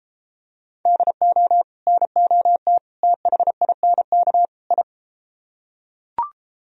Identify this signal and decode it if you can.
Morse code, heard as a keyed tone, about 32 wpm, 700 Hz. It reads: DO NOT THINK I